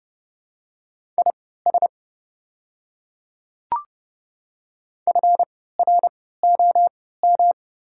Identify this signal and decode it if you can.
Morse code, heard as a keyed tone, about 30 wpm, spaced as for 17 wpm, 700 Hz. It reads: IS FROM
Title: 3x Character Spacing - Top 100 Words - 30wpm